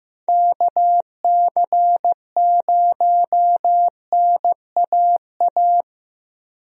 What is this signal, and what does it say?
Morse code, heard as a keyed tone, about 15 wpm, 700 Hz. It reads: KC0NAA
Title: USA Call Signs - 6 Characters - Repeated 3x - 15wpm